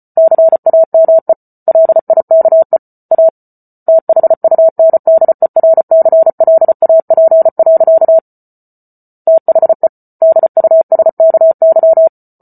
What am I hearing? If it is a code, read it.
Morse code, heard as a keyed tone, about 34 wpm, 650 Hz. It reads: CAME LIKE A THUNDERCLAP. THE DUSKY